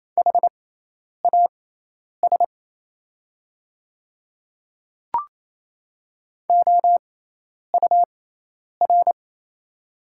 Morse code, heard as a keyed tone, about 28 wpm, spaced as for 10 wpm, 700 Hz. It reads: HAS OUR
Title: 6x Character Spacing - Top 200 Words - 28wpm